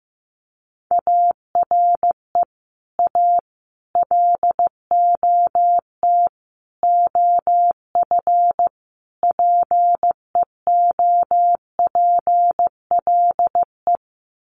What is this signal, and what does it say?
Morse code, heard as a keyed tone, about 15 wpm, 700 Hz. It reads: ARE A LOT OF PEOPLE
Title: Sets of 5 Words 15wpm